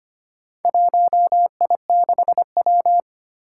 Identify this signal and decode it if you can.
Morse code, heard as a keyed tone, about 25 wpm, 700 Hz. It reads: 1I6W
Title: Four Challenging Letter-Number Combinations 25wpm